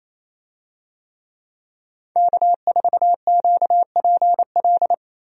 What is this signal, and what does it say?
Morse code, heard as a keyed tone, about 28 wpm, 700 Hz. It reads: K4QPL
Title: USA Call Signs - 5 Characters - Repeated 2x - 28wpm